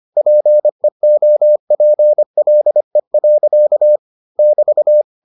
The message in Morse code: PEOPLE. <BT>

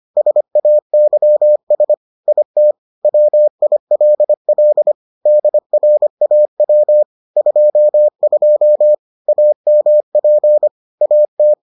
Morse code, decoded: SAYS IT WILL DRAW 22 AMP AT

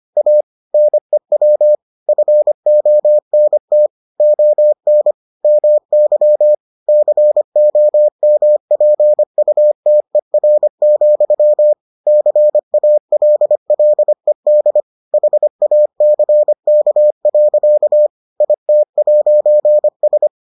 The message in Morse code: A NEW FONT ON MY COMPUTER, CALLED HACK. IT'S

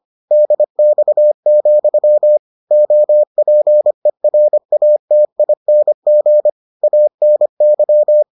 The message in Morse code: DX, OPERATING ANY

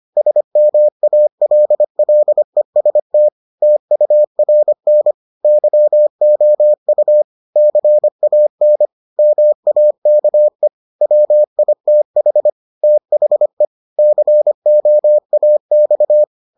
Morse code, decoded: SMALLEST TURN YOU CAN MAKE WITH THE COAX

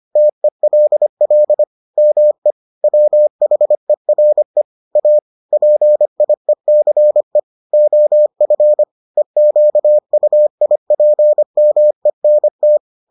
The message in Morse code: TELL ME WHERE A PIECE OF EQUIPMENT

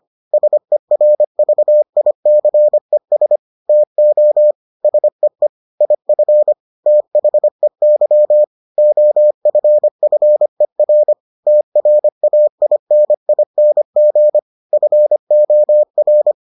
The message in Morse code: SERVICES TO SEE IF THEY OFFER TRAINING FOR

